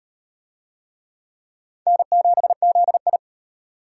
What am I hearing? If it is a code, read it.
Morse code, heard as a keyed tone, about 38 wpm, 700 Hz. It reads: N7ZI